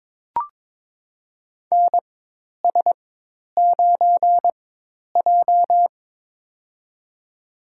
Morse code, heard as a keyed tone, about 22 wpm, spaced as for 10 wpm, 700 Hz. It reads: NS9J
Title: USA Call Signs - 4x Character Spacing - 22wpm